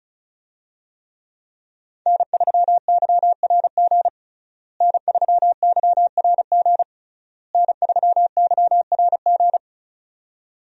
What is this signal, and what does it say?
Morse code, heard as a keyed tone, about 34 wpm, 700 Hz. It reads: N3YRG N3YRG N3YRG
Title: USA Call Signs - 5 Characters - Repeated 3x - 35wpm